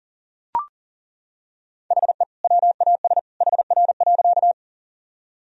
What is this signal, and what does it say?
Morse code, heard as a keyed tone, about 37 wpm, 700 Hz. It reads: HE WAS HR.
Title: Sentences from Top 200 Words and Abbreviations 40wpm